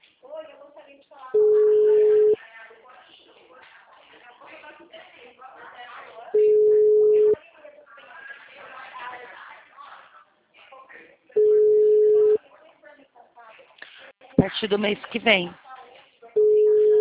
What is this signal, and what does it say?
Ringback tone (Europe, 425 Hz), with speech.
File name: out- -2010- - - .43580